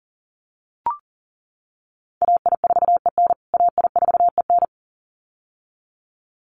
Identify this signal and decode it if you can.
Morse code, heard as a keyed tone, about 40 wpm, 700 Hz. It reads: AI4EN AI4EN
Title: USA Call Signs Repeated 2x 40wpm